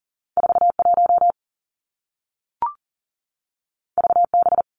Morse code, heard as a keyed tone, about 40 wpm, 700 Hz. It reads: 4 1 V B
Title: Two Challenging Letter-Number Combinations 40wpm